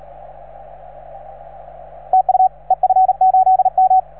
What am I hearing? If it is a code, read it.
Morse code, heard as a keyed tone, about 37 wpm, 750 Hz, with radio noise. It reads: TU EF8M